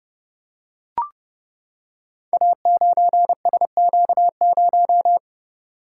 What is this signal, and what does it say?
Morse code, heard as a keyed tone, about 30 wpm, 700 Hz. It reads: A9SQ0
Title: Code Groups - 5 Letter:Numbers - 30wpm